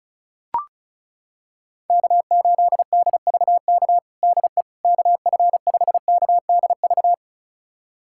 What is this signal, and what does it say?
Morse code, heard as a keyed tone, about 34 wpm, 700 Hz. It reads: K8DVK DE KF5KDV
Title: USA Callsign DE Callsign 35wpm